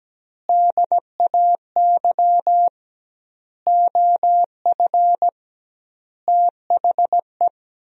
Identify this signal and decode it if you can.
Morse code, heard as a keyed tone, about 17 wpm, 700 Hz. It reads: DAY OF THE
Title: 2x Word-Spacing - Sentences from Top 300 Words - 17wpm